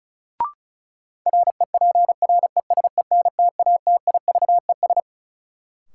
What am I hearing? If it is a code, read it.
Morse code, heard as a keyed tone, about 34 wpm, 700 Hz. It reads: REPRESENTATIVES
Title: Sets of 5 Words Encore - New Word Drill - 35wpm